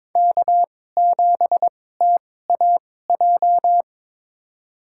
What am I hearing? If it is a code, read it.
Morse code, heard as a keyed tone, about 23 wpm, spaced as for 16 wpm, 700 Hz. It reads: K7TAJ
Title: USA Call Signs - 5 Characters - 2x Character Spacing - 22wpm